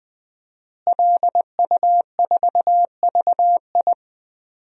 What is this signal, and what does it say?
Morse code, heard as a keyed tone, about 20 wpm, 700 Hz. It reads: LU4VI
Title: DXCC-100 Call Signs - 5 Characters - Repeated 2x - 20wpm